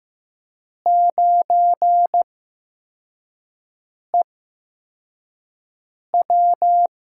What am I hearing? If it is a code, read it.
Morse code, heard as a keyed tone, about 15 wpm, spaced as for 4 wpm, 700 Hz. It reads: 9EW
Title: USA Call Signs - 8x Character Spacing - 15wpm